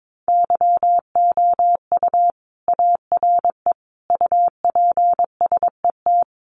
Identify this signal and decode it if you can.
Morse code, heard as a keyed tone, about 22 wpm, 700 Hz. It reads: YOU ARE UPSET
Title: Sentences from Top 800 Words 22wpm